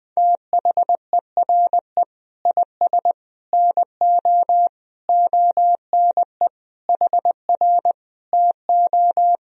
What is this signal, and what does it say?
Morse code, heard as a keyed tone, about 21 wpm, 700 Hz. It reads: THERE IS NO ONE HR TO
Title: Sentences from Top 200 Words and Abbreviations 20wpm